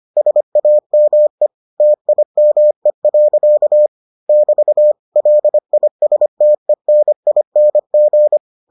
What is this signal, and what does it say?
Morse code, heard as a keyed tone, about 25 wpm, 600 Hz. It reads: SAME TIME. <BT> LISTENING